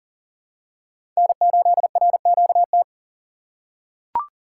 Morse code, heard as a keyed tone, about 39 wpm, 700 Hz. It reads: N8RQT
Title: USA Call Signs - 5 Characters - 3x Character Spacing - 40wpm